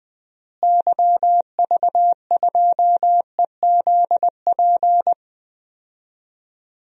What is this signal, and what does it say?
Morse code, heard as a keyed tone, about 20 wpm, 700 Hz. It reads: YV2EZP
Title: DXCC-100 Call Signs - 6 Characters - Repeated 3x - 20wpm